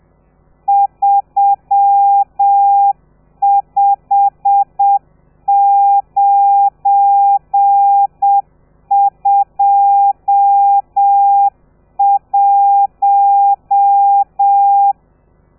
Morse code, heard as a keyed tone, about 7 wpm, 800 Hz. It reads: 35921
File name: telegrafia / i2tfj / Lez12 30